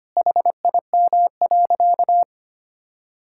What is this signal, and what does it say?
Morse code, heard as a keyed tone, about 25 wpm, 700 Hz. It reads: HIM.
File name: practice / 25wpm / Sentences with 3-Word Phrases - 25wpm